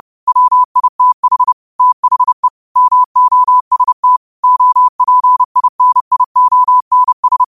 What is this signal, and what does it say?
Morse code, heard as a keyed tone, about 30 wpm, 1000 Hz. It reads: WITH THE MOST OPINIONS
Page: Quotes for Wed, 13 Aug 2025 in Morse Code at 30 words per minute.